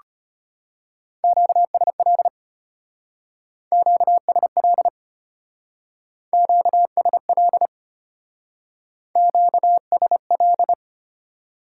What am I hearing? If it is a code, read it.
Morse code, heard as a keyed tone, about 31 wpm, 700 Hz. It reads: QSL QSL QSL QSL
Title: Speed-Racing Top 90 QSO Elements 25wpm